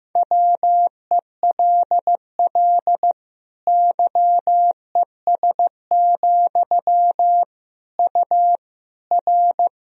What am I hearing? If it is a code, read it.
Morse code, heard as a keyed tone, about 15 wpm, 700 Hz. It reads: WELL YES, U R
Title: Sentences from Top 500 Words and Abbreviations 15wpm